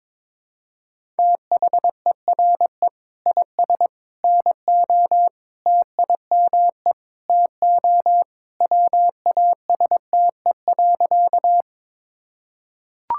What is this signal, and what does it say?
Morse code, heard as a keyed tone, about 22 wpm, 700 Hz. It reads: THERE IS NO TIME TO WASTE.
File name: practice / 22wpm / Sentences with 3-Word Phrases - 22wpm